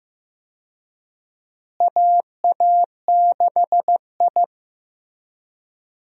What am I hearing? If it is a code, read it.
Morse code, heard as a keyed tone, about 15 wpm, 700 Hz. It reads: AA6I
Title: USA Call Signs - 4 Characters - Repeated 3x - 15wpm